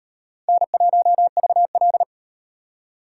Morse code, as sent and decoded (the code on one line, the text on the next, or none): -. .---- ...- .-..
N1VL